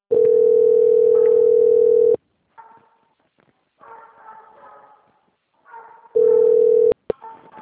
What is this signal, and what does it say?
Ringback tone (North America), 440 + 480 Hz.
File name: out- - - - - .15824